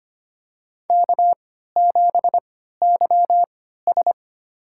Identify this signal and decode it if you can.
Morse code, heard as a keyed tone, about 25 wpm, spaced as for 14 wpm, 700 Hz. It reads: K7YS
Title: USA Call Signs - 4 Characters - 3x Character Spacing - 25wpm